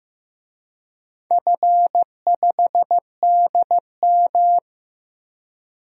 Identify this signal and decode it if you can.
Morse code, heard as a keyed tone, about 15 wpm, 700 Hz. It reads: F5DM
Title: DXCC-100 Call Signs - 4 Characters - Repeated 3x - 15wpm